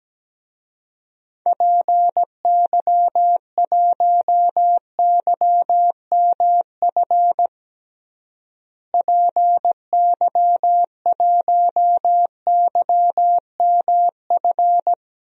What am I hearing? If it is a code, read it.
Morse code, heard as a keyed tone, about 17 wpm, 700 Hz. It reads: PY1YMF PY1YMF
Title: DXCC-100 Call Signs - 6 Characters - Repeated 2x - 17wpm